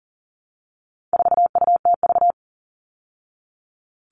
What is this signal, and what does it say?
Morse code, heard as a keyed tone, about 39 wpm, 700 Hz. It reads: 4UTV
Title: All Cut Numbers - 4-Digit 40wpm